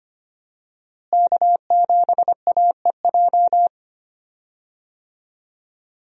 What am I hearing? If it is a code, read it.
Morse code, heard as a keyed tone, about 25 wpm, 700 Hz. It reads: K7AEJ
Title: USA Call Signs - 5 Characters - Repeated 2x - 25wpm